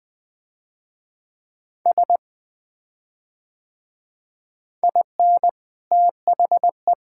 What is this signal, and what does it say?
Morse code, heard as a keyed tone, about 20 wpm, 700 Hz. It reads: S IN THE
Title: Build A Phrase - 3-Word Phrases - 20wpm